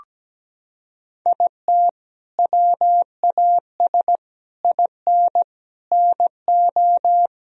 Morse code, heard as a keyed tone, about 17 wpm, 700 Hz. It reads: IT WAS IN NO
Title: Sentences from Top 1000 Words - Review Part 4 - 17wpm